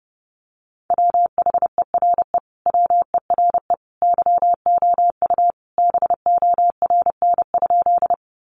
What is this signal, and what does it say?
Morse code, heard as a keyed tone, about 30 wpm, 700 Hz. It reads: WHERE WERE YOU BORN?
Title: Sentences from Top 1000 Words 30wpm